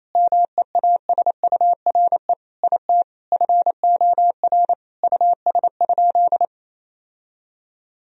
Morse code, heard as a keyed tone, about 28 wpm, 700 Hz. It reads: MEASURE IT FOR US?
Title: Sentences from Top 1000 Words - Review Part 2 - 28wpm